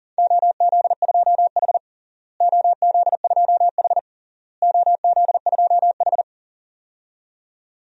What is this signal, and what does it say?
Morse code, heard as a keyed tone, about 40 wpm, 700 Hz. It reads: OZ2H OZ2H OZ2H
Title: DXCC-100 Call Signs - 4 Characters - Repeated 3x - 40wpm